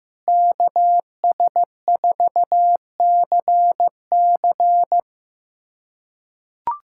Morse code, heard as a keyed tone, about 16 wpm, 700 Hz. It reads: KS4CC
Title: USA Callsign DE Callsign 15wpm